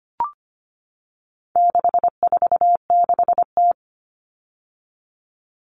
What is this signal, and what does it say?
Morse code, heard as a keyed tone, about 25 wpm, 700 Hz. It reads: 646T
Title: All Cut Numbers - 4-Digit 25wpm